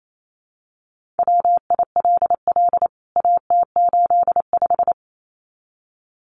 Morse code, heard as a keyed tone, about 28 wpm, 700 Hz. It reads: WILL AT85
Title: CWT Exchange 28wpm